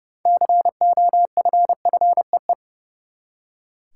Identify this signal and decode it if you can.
Morse code, heard as a keyed tone, about 30 wpm, 700 Hz. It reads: COFFEE
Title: Six Letter Words - 30wpm